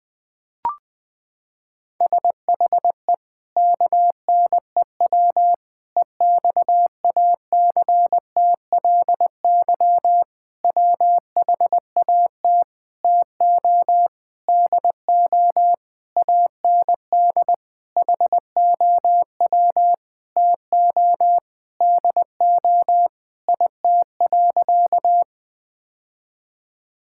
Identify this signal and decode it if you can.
Morse code, heard as a keyed tone, about 20 wpm, 700 Hz. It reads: SHE KNEW EXACTLY WHAT TO DO AND HOW TO DO IT.
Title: Sentences with 8-Word Phrases - 20wpm